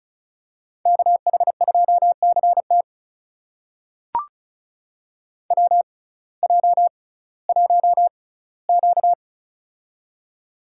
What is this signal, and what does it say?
Morse code, heard as a keyed tone, about 36 wpm, spaced as for 19 wpm, 700 Hz. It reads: KH2CT WJ1Q
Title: USA Call Signs - 6x Character Spacing - 35wpm